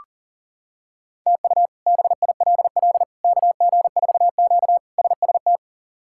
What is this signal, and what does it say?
Morse code, heard as a keyed tone, about 40 wpm, 700 Hz. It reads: TU BILL KG4Q SST
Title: K1USN SST QSO Practice - 40wpm